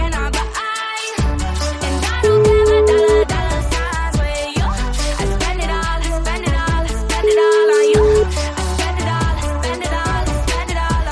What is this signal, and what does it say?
Ringback tone (Europe, 425 Hz), with noise.